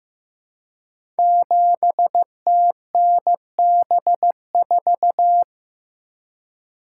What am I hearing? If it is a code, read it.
Morse code, heard as a keyed tone, about 15 wpm, 700 Hz. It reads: 7TNB4
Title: Code Groups - 5 Letter:Numbers - 15wpm